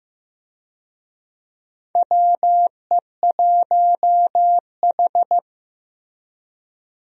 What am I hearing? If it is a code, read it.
Morse code, heard as a keyed tone, about 15 wpm, 700 Hz. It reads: WE1H
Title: USA Call Signs - 4 Characters - Repeated 3x - 15wpm